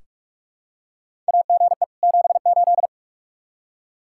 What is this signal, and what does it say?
Morse code, heard as a keyed tone, about 42 wpm, 700 Hz. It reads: AGE 78
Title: Practice QSO - Exchange 3 - More Info - 45wpm